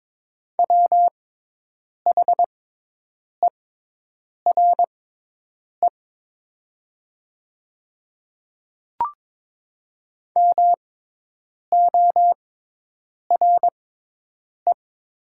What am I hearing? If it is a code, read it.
Morse code, heard as a keyed tone, about 22 wpm, spaced as for 8 wpm, 700 Hz. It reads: WHERE MORE
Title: 6x Character Spacing - Top 300 Words - 22wpm